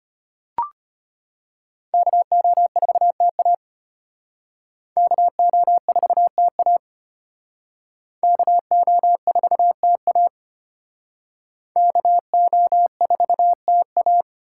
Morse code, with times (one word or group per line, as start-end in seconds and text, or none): 1.9-3.5 KO4TA
5.0-6.8 KO4TA
8.2-10.3 KO4TA
11.8-14.2 KO4TA